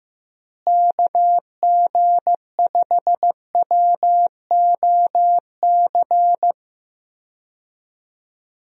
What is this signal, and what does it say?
Morse code, heard as a keyed tone, about 15 wpm, 700 Hz. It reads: KG5WOC
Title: USA Call Signs - 6 Characters - Repeated 3x - 15wpm